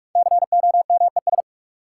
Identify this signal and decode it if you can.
Morse code, heard as a keyed tone, about 43 wpm, 700 Hz. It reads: COMES
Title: Sentences from Top 1000 Words - Review Part 5 - New Word Drill - 45wpm